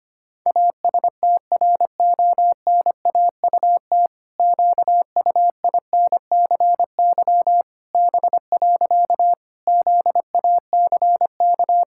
Morse code, heard as a keyed tone, about 25 wpm, 700 Hz. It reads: ASTRONAUT QUINCY B. ZACK
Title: Pangram Challenge v2 - 25wpm